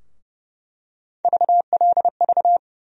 Morse code, heard as a keyed tone, about 30 wpm, 700 Hz. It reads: VLV
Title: USA Call Sign Postfixes - 3 Letters 30wpm